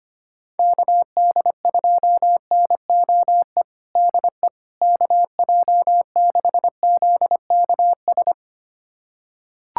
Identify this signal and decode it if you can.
Morse code, heard as a keyed tone, about 25 wpm, 700 Hz. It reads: KD2NOE DE KJ6ZKS
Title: USA Callsign DE Callsign - 6 Characters - 25wpm